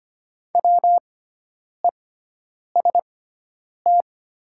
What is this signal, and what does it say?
Morse code, heard as a keyed tone, about 25 wpm, spaced as for 9 wpm, 690 Hz. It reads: WEST
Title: 6x Character Spacing - Top 500 Words - 25wpm